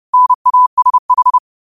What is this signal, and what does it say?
Morse code, heard as a keyed tone, about 30 wpm, 1000 Hz. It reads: NASH